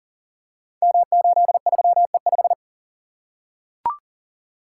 Morse code, heard as a keyed tone, about 38 wpm, 700 Hz. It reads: M83E5
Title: Code Groups - 5 Letter:Numbers - 40wpm